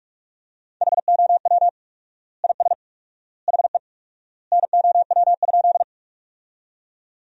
Morse code, heard as a keyed tone, about 46 wpm, 700 Hz. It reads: HOW IS HE NOW?
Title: 4x Word-Spacing - Sentences from Top 100 Words - 45wpm